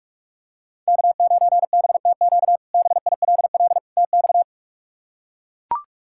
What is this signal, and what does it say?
Morse code, heard as a keyed tone, about 42 wpm, 700 Hz. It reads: K9BTQ BILL TX
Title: K1USN SST QSO Practice - 45wpm